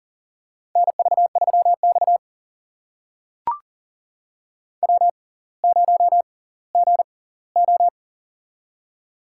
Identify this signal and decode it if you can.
Morse code, heard as a keyed tone, about 40 wpm, spaced as for 21 wpm, 700 Hz. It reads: NV3X W0GO
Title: USA Call Signs - 4 Characters - 6x Character Spacing - 40wpm